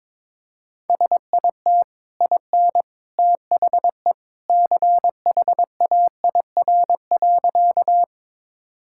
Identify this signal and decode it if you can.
Morse code, heard as a keyed tone, about 22 wpm, 700 Hz. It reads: SIT IN THE CHAIR.
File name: practice / 22wpm / Easy Sentences with Hints - 22wpm